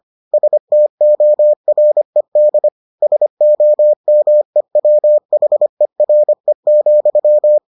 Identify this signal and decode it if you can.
Morse code, heard as a keyed tone, about 25 wpm, 600 Hz. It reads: STORED SOMEWHERE,